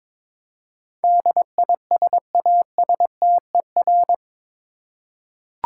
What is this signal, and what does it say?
Morse code, heard as a keyed tone, about 22 wpm, 700 Hz. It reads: DISASTER
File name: practice / 22wpm / Eight Letter Words - 22wpm